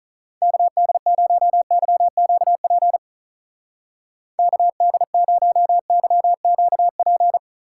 Morse code, heard as a keyed tone, about 44 wpm, 700 Hz. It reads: KD0YQP KD0YQP